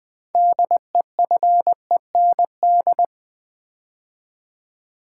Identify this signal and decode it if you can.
Morse code, heard as a keyed tone, about 20 wpm, 700 Hz. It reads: DEFEND
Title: 6 Letter Words - First 3 Letters Given - 20wpm